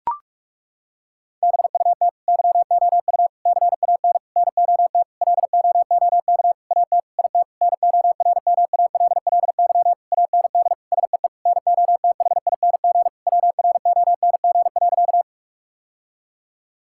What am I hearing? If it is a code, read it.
Morse code, heard as a keyed tone, about 41 wpm, 700 Hz. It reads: BUT YOU CAN NOT LOOK AT IT NORMALLY AND SEE NOTHING WRONG.
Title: Sentences from Top 1000 Words - Review Part 6 - 45wpm